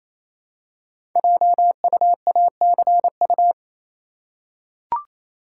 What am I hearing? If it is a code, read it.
Morse code, heard as a keyed tone, about 29 wpm, 700 Hz. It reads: JUACU
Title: Code Groups - 5 Letters - 28wpm